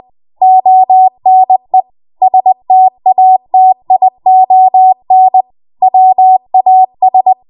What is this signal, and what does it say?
Morse code, heard as a keyed tone, about 20 wpm, 750 Hz. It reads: ONE STATION WAS
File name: Morse / 20WPM